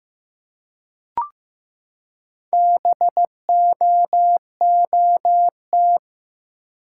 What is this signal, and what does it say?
Morse code, heard as a keyed tone, about 15 wpm, 700 Hz. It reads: BOOT
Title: Four Letter Words - 15wpm